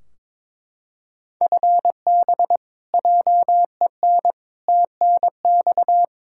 Morse code, heard as a keyed tone, about 22 wpm, 700 Hz. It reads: FB JEN TNX
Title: Practice QSO - Exchange 3 - More Info - 22wpm